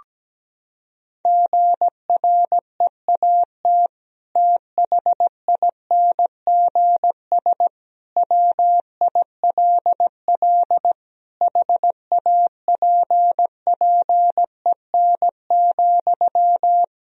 Morse code, heard as a keyed tone, about 17 wpm, 700 Hz. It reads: GREAT THINGS WILL HAPPEN,
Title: Sentences from Top 1000 Words - Review Part 4 - 17wpm